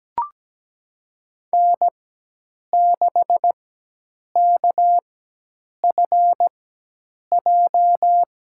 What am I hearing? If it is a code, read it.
Morse code, heard as a keyed tone, about 17 wpm, spaced as for 8 wpm, 700 Hz. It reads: N6KFJ
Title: USA Call Signs - 5 Characters - 4x Character Spacing - 17wpm